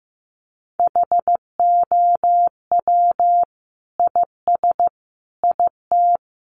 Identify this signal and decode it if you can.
Morse code, heard as a keyed tone, about 15 wpm, 700 Hz. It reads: HOW IS IT
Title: Sentences from Top 900 Words 15wpm